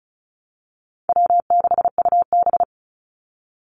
Morse code, heard as a keyed tone, about 34 wpm, 700 Hz. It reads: W6UB
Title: NA SOTA Activator Callsigns 35wpm